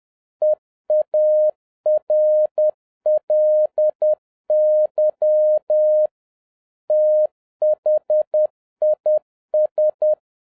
Morse code, heard as a keyed tone, about 10 wpm, 610 Hz. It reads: EARLY THIS